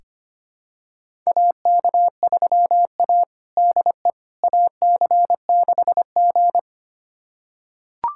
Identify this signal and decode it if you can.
Morse code, heard as a keyed tone, about 25 wpm, 700 Hz. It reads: AK3A DE AC6G
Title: USA Callsign DE Callsign - 4 Characters - 25wpm